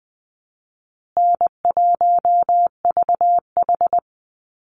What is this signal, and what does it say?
Morse code, heard as a keyed tone, about 20 wpm, 700 Hz. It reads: N1VH
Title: Easy Call Signs Repeated 3x - 20wpm